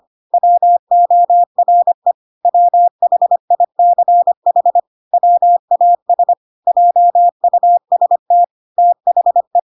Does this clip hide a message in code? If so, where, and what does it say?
Yes: Morse code from 0.3 to 9.6 s, heard as a keyed tone, about 25 wpm, 700 Hz. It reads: WORE WHICH WAS JUST THE